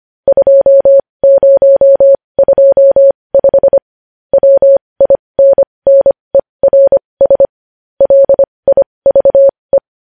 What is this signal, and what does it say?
Morse code, heard as a keyed tone, about 25 wpm, spaced as for 20 wpm, 550 Hz. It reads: 2025 WINNERS LIVE
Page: news-headlines-in-morse-code-at-20-wpm